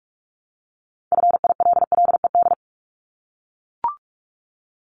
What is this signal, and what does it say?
Morse code, heard as a keyed tone, about 44 wpm, 700 Hz. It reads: FILLED
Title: ED Suffix - Top 100 Words with Base Words 45wpm